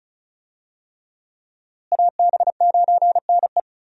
Morse code, heard as a keyed tone, about 36 wpm, 700 Hz. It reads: AB9NE
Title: USA Call Signs - 5 Characters - Repeated 3x - 35wpm